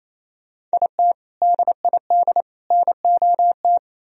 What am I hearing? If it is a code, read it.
Morse code, heard as a keyed tone, about 33 wpm, 700 Hz. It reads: IT DID NOT
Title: Speed-Racing - Sets of 3 Words 28wpm